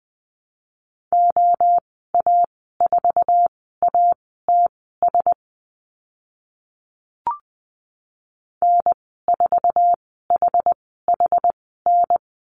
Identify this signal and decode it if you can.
Morse code, heard as a keyed tone, about 20 wpm, spaced as for 15 wpm, 700 Hz. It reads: OA4ATS N4HHN
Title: 082 Lesson 14g - Number 2 - Call Signs 20wpm